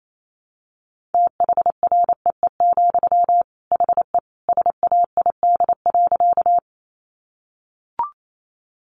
Morse code, heard as a keyed tone, about 28 wpm, 700 Hz. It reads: THREE, HE SAID.
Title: Sentences from Top 200 Words 28wpm